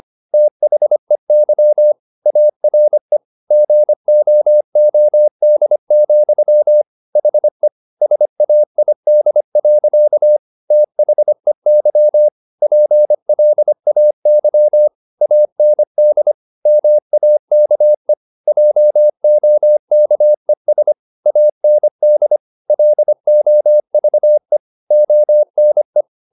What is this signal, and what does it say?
Morse code, heard as a keyed tone, about 25 wpm, 600 Hz. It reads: THEY ARE GOOD, HE SAID. THEY PLAY AND MAKE JOKES AND LOVE ONE
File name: the old man and the sea part 1 25wpm